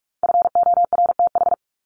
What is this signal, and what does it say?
Morse code, heard as a keyed tone, about 44 wpm, 700 Hz. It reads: FORTH
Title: Top 1751-2000 Words 45wpm